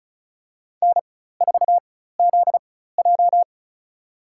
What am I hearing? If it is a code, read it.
Morse code, heard as a keyed tone, about 36 wpm, spaced as for 16 wpm, 700 Hz. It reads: N4ZJ
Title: USA Call Signs - 4 Characters - 4x Character Spacing - 35wpm